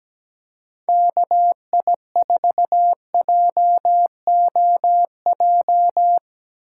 Morse code, heard as a keyed tone, about 17 wpm, 700 Hz. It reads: KI4JOJ
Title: USA Call Signs - 6 Characters - 2x Character Spacing - 17wpm